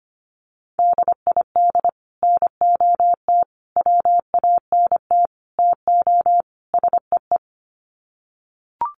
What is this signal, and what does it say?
Morse code, heard as a keyed tone, about 25 wpm, 700 Hz. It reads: DID NOT WANT TO SEE
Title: Sets of 5 Words Encore2 25wpm